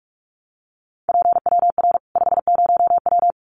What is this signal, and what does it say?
Morse code, heard as a keyed tone, about 44 wpm, 700 Hz. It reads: PWR 50W
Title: Common QSO Phrases 45wpm